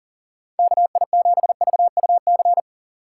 Morse code, heard as a keyed tone, about 49 wpm, 700 Hz. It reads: KI7VUC
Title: Speed Racing - USA 2x3 Call Signs 40wpm